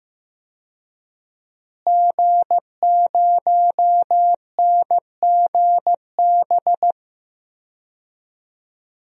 Morse code, heard as a keyed tone, about 15 wpm, 700 Hz. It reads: G0NGB
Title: DXCC-100 Call Signs - 5 Characters - Repeated 3x - 15wpm